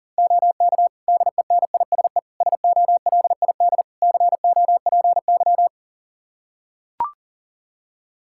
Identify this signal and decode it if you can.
Morse code, heard as a keyed tone, about 40 wpm, 700 Hz. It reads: OK DENISE SOLID COPY
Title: K1USN SST QSO Practice 40wpm